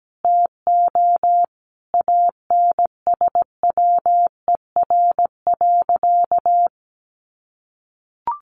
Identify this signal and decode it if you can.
Morse code, heard as a keyed tone, about 17 wpm, 700 Hz. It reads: TO ANSWER.
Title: Sentences from Top 1500 Words 17wpm